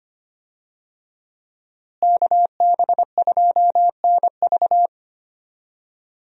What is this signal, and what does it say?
Morse code, heard as a keyed tone, about 25 wpm, 700 Hz. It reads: KB2NV
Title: USA Call Signs - 5 Characters - Repeated 2x - 25wpm